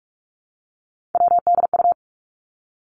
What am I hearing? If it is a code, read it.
Morse code, heard as a keyed tone, about 46 wpm, 700 Hz. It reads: WDU
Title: Three Challenging Letter-Number Combinations 45wpm